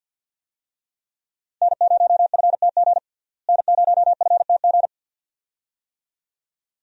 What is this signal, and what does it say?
Morse code, heard as a keyed tone, about 47 wpm, 700 Hz. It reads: N0FTG N0FTG
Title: USA Call Signs - 5 Characters - Repeated 2x - 50wpm